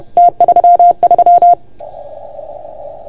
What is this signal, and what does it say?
Morse code, heard as a keyed tone, about 30 wpm, 690 Hz. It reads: T33
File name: Archiv / CUC381 / OK1FGD CUC 381